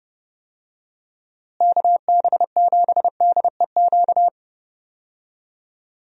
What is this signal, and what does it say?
Morse code, heard as a keyed tone, about 30 wpm, 700 Hz. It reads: KB7DEQ
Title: USA Call Signs - 6 Characters - Repeated 2x - 30wpm